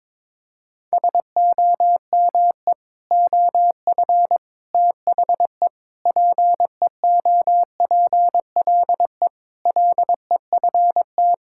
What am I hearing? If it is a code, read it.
Morse code, heard as a keyed tone, about 22 wpm, 700 Hz. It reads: SOME OF THE PEOPLE LEFT
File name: practice / 22wpm / Sentences with 3-Word Phrases - 22wpm